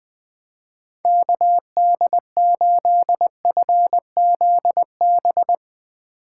Morse code, heard as a keyed tone, about 21 wpm, 700 Hz. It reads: KD8FZB
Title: USA Call Signs - 6 Characters - 4x Character Spacing - 20wpm